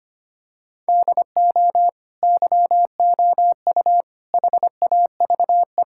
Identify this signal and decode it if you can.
Morse code, heard as a keyed tone, about 25 wpm, 700 Hz. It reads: DO YOU HAVE
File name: practice / 25wpm / Sentences with 2-Word Phrases - 25wpm